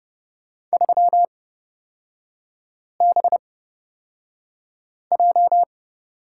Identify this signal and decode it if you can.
Morse code, heard as a keyed tone, about 29 wpm, spaced as for 5 wpm, 700 Hz. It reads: 3BJ
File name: Lesson 36e - BT - Three Characters 30wpm